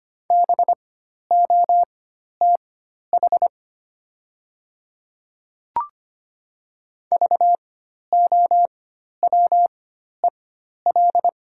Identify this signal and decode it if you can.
Morse code, heard as a keyed tone, about 25 wpm, spaced as for 12 wpm, 700 Hz. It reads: BOTH VOWEL
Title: 4x Character Spacing - Top 500 Words - 25wpm